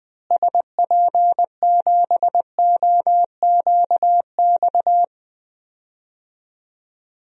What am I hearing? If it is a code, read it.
Morse code, heard as a keyed tone, about 20 wpm, 700 Hz. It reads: SP7OQX
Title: DXCC-100 Call Signs - 6 Characters - Repeated 2x - 20wpm